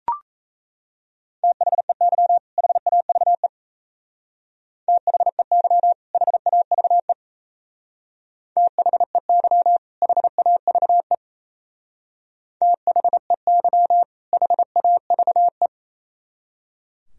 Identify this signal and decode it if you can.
Morse code, heard as a keyed tone, about 33 wpm, 700 Hz. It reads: THEY HAVE THEY HAVE THEY HAVE THEY HAVE
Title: Speed-Racing - Sets of 2 Words 28wpm